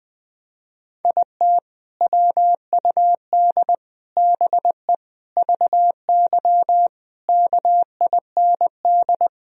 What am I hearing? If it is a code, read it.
Morse code, heard as a keyed tone, about 20 wpm, 700 Hz. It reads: IT WUD BE VY KIND
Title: Sentences from Top 200 Words and Abbreviations 20wpm